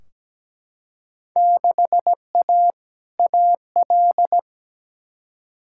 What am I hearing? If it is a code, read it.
Morse code, heard as a keyed tone, about 17 wpm, 700 Hz. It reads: 6A AL
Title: Field Day Exchanges Encore - 17wpm